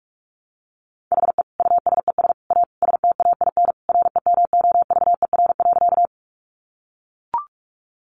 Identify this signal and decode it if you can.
Morse code, heard as a keyed tone, about 44 wpm, 700 Hz. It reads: HE USES A STAIN REMOVER.
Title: Additional Easy Phrases 45wpm